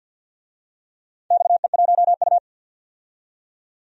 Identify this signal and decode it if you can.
Morse code, heard as a keyed tone, about 53 wpm, 700 Hz. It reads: XE1U
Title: DXCC-100 Call Signs - 4 Characters - 50wpm